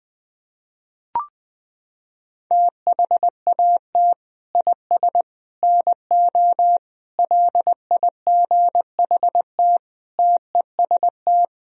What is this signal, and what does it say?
Morse code, heard as a keyed tone, about 20 wpm, 700 Hz. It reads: THAT IS NO LIGHT TEST
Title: Sentences from Top 1000 Words - Review Part 3 - 20wpm